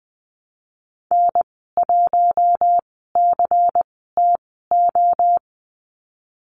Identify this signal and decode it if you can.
Morse code, heard as a keyed tone, about 20 wpm, spaced as for 15 wpm, 700 Hz. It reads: N1CTO
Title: 173 Lesson 28g - Number 9 - Call Signs 20wpm